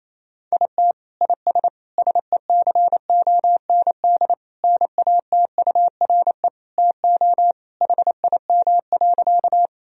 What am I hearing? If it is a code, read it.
Morse code, heard as a keyed tone, about 28 wpm, 700 Hz. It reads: IT IS SECOND NATURE TO HIM.
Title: Sentences from Top 1000 Words - Review Part 1 - 28wpm